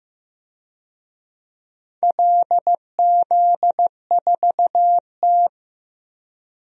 Morse code, heard as a keyed tone, about 15 wpm, 700 Hz. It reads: LZ4T
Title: DXCC-100 Call Signs - 4 Characters - Repeated 3x - 15wpm